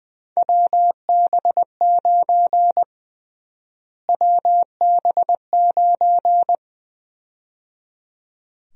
Morse code, heard as a keyed tone, about 20 wpm, 700 Hz. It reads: WB9 WB9
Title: USA Call Sign Prefixes - Repeated 2x - 20wpm